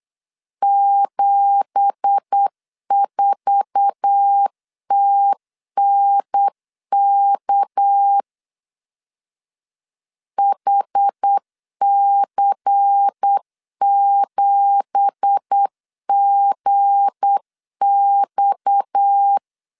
Morse code, heard as a keyed tone, about 8 wpm, 800 Hz. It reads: 74TNK HC7GX
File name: 8wpm